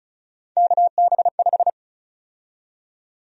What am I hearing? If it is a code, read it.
Morse code, heard as a keyed tone, about 36 wpm, 700 Hz. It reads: KB5
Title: USA Call Sign Prefixes - 35wpm